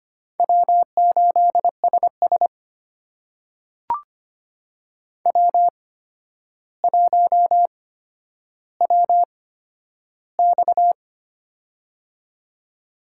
Morse code, heard as a keyed tone, about 25 wpm, spaced as for 11 wpm, 700 Hz. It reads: W8SS W1WX
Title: USA Call Signs - 4 Characters - 8x Character Spacing - 25wpm